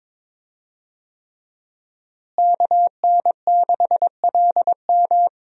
Morse code, heard as a keyed tone, about 22 wpm, 700 Hz. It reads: KN6LM
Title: USA Call Signs - 5 Characters - Repeated 3x - 22wpm